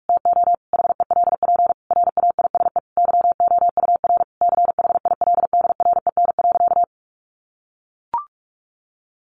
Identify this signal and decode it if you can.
Morse code, heard as a keyed tone, about 44 wpm, 700 Hz. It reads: TO HELP RAISE YOUR CHILDREN.
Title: Sentences from Top 1400 Words 45wpm